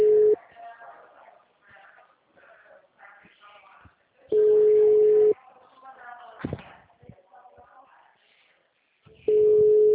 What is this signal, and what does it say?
Ringback tone (Europe, 425 Hz), with speech.
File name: out- -2017- - - .9305